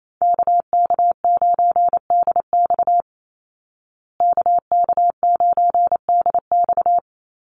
Morse code, heard as a keyed tone, about 28 wpm, 700 Hz. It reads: KK9DX KK9DX
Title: Call Signs Repeated 3x - 28wpm